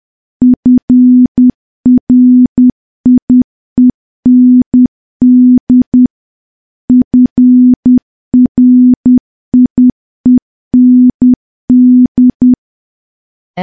Morse code, heard as a keyed tone, about 10 wpm, 250 Hz. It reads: FRIEND FRIEND
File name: morse / morse-training-10WPM